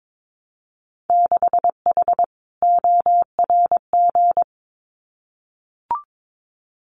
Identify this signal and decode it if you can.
Morse code, heard as a keyed tone, about 22 wpm, 700 Hz. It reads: 6H ORG
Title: Winter Field Day Exchanges Encore 22wpm